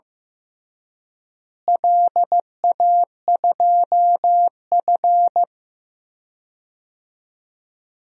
Morse code, heard as a keyed tone, about 15 wpm, 700 Hz. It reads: LA2F
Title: DXCC-100 Call Signs - 4 Characters - Repeated 2x - 15wpm